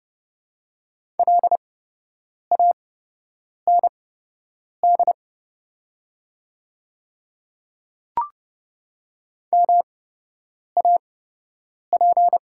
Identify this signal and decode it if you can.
Morse code, heard as a keyed tone, about 30 wpm, spaced as for 8 wpm, 700 Hz. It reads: LAND MAP
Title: 8x Character Spacing - Top 500 Words - 30wpm